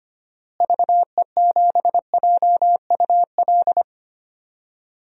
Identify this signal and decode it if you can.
Morse code, heard as a keyed tone, about 25 wpm, 700 Hz. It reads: VE7JUL
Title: DXCC-100 Call Signs - 6 Characters - Repeated 3x - 25wpm